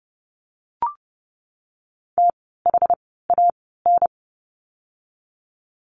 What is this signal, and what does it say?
Morse code, heard as a keyed tone, about 30 wpm, spaced as for 17 wpm, 700 Hz. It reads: THAN
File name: course / 30:10wpm / 237 Lesson 38f - Letter q - Words 30wpm